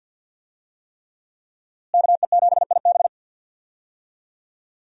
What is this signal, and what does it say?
Morse code, heard as a keyed tone, about 47 wpm, 700 Hz. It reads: KE7IB
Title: USA Call Signs - 5 Characters - Repeated 3x - 50wpm